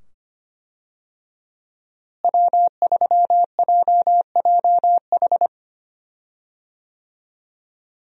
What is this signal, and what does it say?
Morse code, heard as a keyed tone, about 25 wpm, 700 Hz. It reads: W3JJH
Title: USA Call Signs - 5 Characters - Repeated 2x - 25wpm